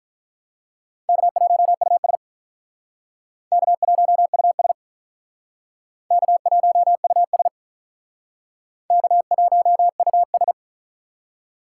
Morse code, heard as a keyed tone, about 46 wpm, 700 Hz. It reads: K1US K1US K1US K1US